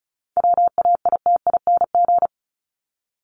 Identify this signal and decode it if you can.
Morse code, heard as a keyed tone, about 34 wpm, 700 Hz. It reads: WAITING
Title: ING Suffix - Top 100 Words with Base Words 35wpm